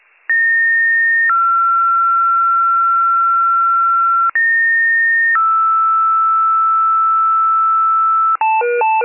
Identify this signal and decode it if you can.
Two-tone page, heard as a one-second tone, then a three-second tone, about 1850 then 1360 Hz, about 1850 then 1320 Hz. It is heard from 0.3 to 8.4 s.